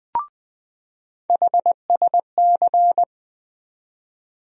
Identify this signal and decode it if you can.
Morse code, heard as a keyed tone, about 20 wpm, 700 Hz. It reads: HSC
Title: USA Call Sign Postfixes - 3 Letters 20wpm